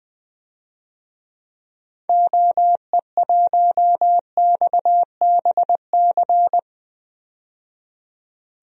Morse code, heard as a keyed tone, about 21 wpm, 700 Hz. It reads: OE1XBC
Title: DXCC-100 Call Signs - 6 Characters - 20wpm